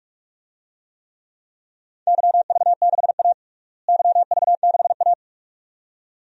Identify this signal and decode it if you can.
Morse code, heard as a keyed tone, about 46 wpm, 700 Hz. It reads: YV6A YV6A
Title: DXCC-100 Call Signs - 4 Characters - Repeated 2x - 45wpm